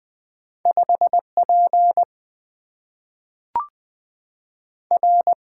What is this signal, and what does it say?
Morse code, heard as a keyed tone, about 20 wpm, 700 Hz. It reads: 5P R
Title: Single-Letter Single-Number 20wpm